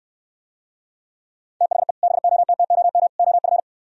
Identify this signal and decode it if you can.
Morse code, heard as a keyed tone, about 69 wpm, 700 Hz. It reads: THE BOTTOM OF